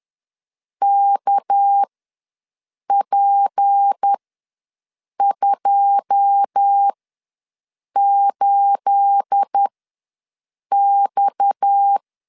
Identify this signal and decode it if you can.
Morse code, heard as a keyed tone, about 10 wpm, spaced as for 6 wpm, 800 Hz. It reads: KP28X
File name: LEZIONI / 9 28 ws 10wpm 4124